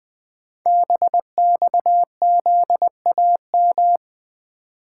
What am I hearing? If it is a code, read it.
Morse code, heard as a keyed tone, about 21 wpm, 700 Hz. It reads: BXZAM
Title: Code Groups - 5 Letters - 20wpm